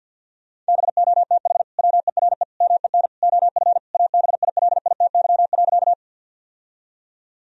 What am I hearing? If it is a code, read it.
Morse code, heard as a keyed tone, about 50 wpm, 700 Hz. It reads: BOTH WERE MEN OF ABILITY.
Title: Sentences from Top 1000 Words - Review Part 3 - 50wpm